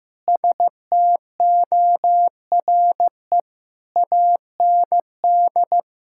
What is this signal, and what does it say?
Morse code, heard as a keyed tone, about 15 wpm, 700 Hz. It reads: STORE AND
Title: Sentences from Top 1000 Words - Review Part 5 - 15wpm